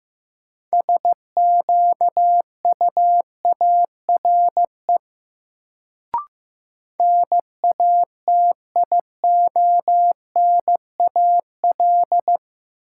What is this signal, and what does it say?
Morse code, heard as a keyed tone, about 15 wpm, 700 Hz. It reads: SQUARE NATIONAL
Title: Sentences from Top 1200 Words - New Word Drill Alt - 15wpm